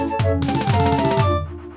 There is no code.